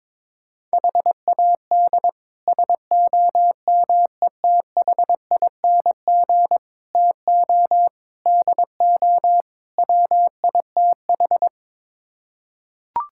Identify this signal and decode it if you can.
Morse code, heard as a keyed tone, about 23 wpm, 700 Hz. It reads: HAD SOMETHING TO DO WITH
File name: practice / 22wpm / Build A Phrase - 5-Word Phrases - 22wpm